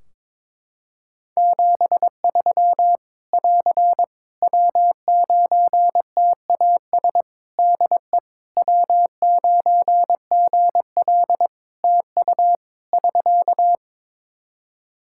Morse code, heard as a keyed tone, about 22 wpm, 700 Hz. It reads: 73 <AR> W9TAS DE W9GL TU <SK>
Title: Practice QSO - Exchange 4 - Ending - 22wpm